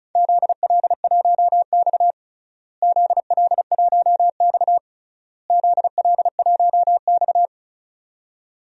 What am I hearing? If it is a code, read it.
Morse code, heard as a keyed tone, about 34 wpm, 700 Hz. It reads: ZL1X ZL1X ZL1X
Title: DXCC-100 Call Signs - 4 Characters - Repeated 3x - 35wpm